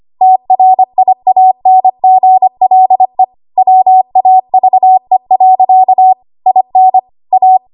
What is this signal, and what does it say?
Morse code, heard as a keyed tone, about 25 wpm, 750 Hz. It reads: TRIANGLE WAVE. IN A
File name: Archive / 25 WPM / 25WPM